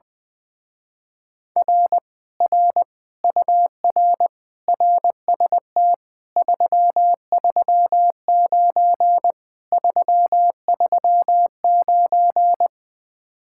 Morse code, heard as a keyed tone, about 20 wpm, 700 Hz. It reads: R R UR RST 339 339
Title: Speed-Racing Common QSO Phrases - 20wpm